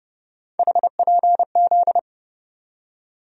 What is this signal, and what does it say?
Morse code, heard as a keyed tone, about 30 wpm, 700 Hz. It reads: HPZ
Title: USA Call Sign Postfixes - 3 Letters 30wpm